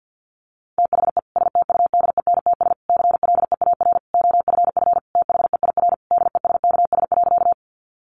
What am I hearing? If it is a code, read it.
Morse code, heard as a keyed tone, about 50 wpm, 700 Hz. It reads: THE STUDENTS CLEAR OFF THEIR DESKS.
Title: Additional Easy Phrases 50wpm